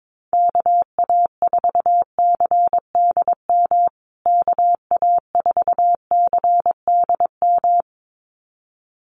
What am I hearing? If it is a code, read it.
Morse code, heard as a keyed tone, about 22 wpm, 700 Hz. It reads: KA4CDM KA4CDM
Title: USA Call Signs Repeated 2x 22wpm